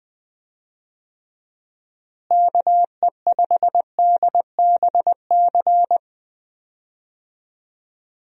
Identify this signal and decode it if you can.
Morse code, heard as a keyed tone, about 20 wpm, 700 Hz. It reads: KE5DBC
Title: USA Call Signs - 6 Characters - Repeated 2x - 20wpm